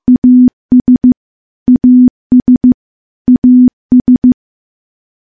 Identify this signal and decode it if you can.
Morse code, heard as a keyed tone, about 15 wpm, 250 Hz. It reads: AS AS AS
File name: morse-training-15WPM